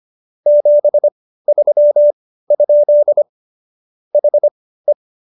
Morse code, heard as a keyed tone, about 25 wpm, spaced as for 15 wpm, 590 Hz. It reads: 73? HE